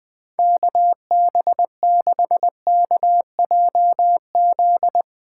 Morse code, heard as a keyed tone, about 20 wpm, 700 Hz. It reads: KB6KJZ
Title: Speed Racing - USA 2x3 Call Signs 20wpm